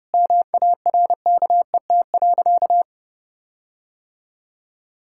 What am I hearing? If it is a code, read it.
Morse code, heard as a keyed tone, about 30 wpm, 700 Hz. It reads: MARKET.
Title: Sentences from Top 1600 Words 30wpm